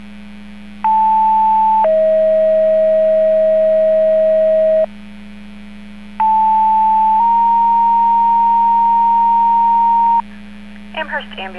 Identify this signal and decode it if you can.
Two-tone page, heard as a one-second tone, then a three-second tone, about 880 then 640 Hz, about 880 then 930 Hz, with speech. It is heard from 0.8 to 10.2 s.